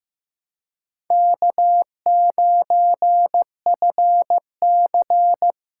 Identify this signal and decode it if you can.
Morse code, heard as a keyed tone, about 15 wpm, 700 Hz. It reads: K9FC
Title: USA Call Signs - 2x Character Spacing - 15wpm